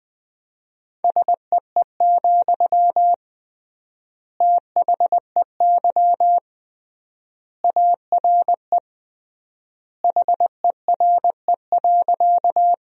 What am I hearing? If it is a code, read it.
Morse code, heard as a keyed tone, about 20 wpm, 700 Hz. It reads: SEE, THEY ARE HERE.
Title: 3x Word-Spacing - Sentences from Top 200 Words - 20wpm